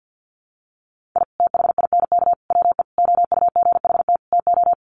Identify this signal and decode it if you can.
Morse code, heard as a keyed tone, about 50 wpm, 700 Hz. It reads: I THINK WE OUGHT TO
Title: Sets of 5 Words Encore 50wpm